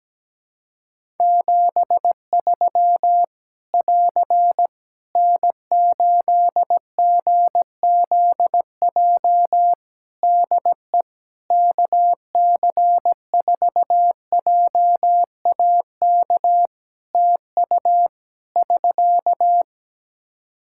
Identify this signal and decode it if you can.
Morse code, heard as a keyed tone, about 17 wpm, 700 Hz. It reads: 73 <AR> N8GZJ DE KC4JAK TU <SK>
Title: Practice QSO - Exchange 4 - Ending - 17wpm